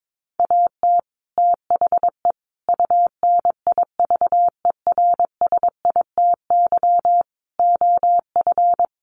Morse code, heard as a keyed tone, about 22 wpm, 700 Hz. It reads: AT THE UNIVERSITY OF
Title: Sets of 5 Words Encore2 22wpm